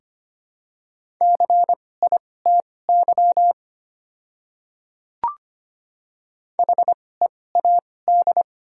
Morse code, heard as a keyed tone, about 25 wpm, spaced as for 18 wpm, 700 Hz. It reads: CITY HEAD
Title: 2x Character Spacing - Top 500 Words - 25wpm